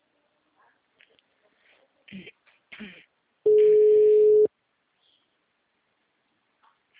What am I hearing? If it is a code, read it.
Ringback tone (Europe, 425 Hz).